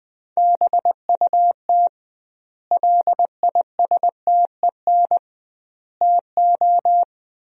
Morse code, heard as a keyed tone, about 20 wpm, 700 Hz. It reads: BUT LISTEN TO
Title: 2x Word-Spacing - Sentences from Top 400 Words - 20wpm